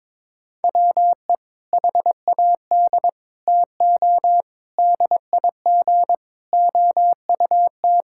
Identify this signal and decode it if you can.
Morse code, heard as a keyed tone, about 22 wpm, 700 Hz. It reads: WE HAD TO DIG OUT
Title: Sentences from Top 1600 Words 22wpm